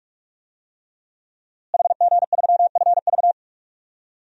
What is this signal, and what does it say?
Morse code, heard as a keyed tone, about 46 wpm, 700 Hz. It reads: HG3FV
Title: DXCC-100 Call Signs - 5 Characters - Repeated 3x - 45wpm